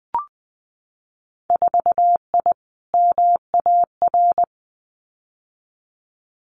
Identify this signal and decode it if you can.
Morse code, heard as a keyed tone, about 20 wpm, 700 Hz. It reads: 4I MAR
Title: Winter Field Day Exchanges Encore 20wpm